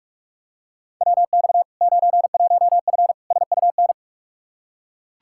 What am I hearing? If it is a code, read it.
Morse code, heard as a keyed tone, about 42 wpm, 700 Hz. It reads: WX 91F SUN
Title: Practice QSO - Exchange 2 - Info - 45wpm